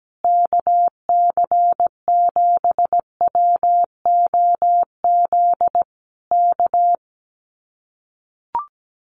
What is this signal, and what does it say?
Morse code, heard as a keyed tone, about 17 wpm, 700 Hz. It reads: KC7WOZ K
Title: Common QSO Phrases Encore 17wpm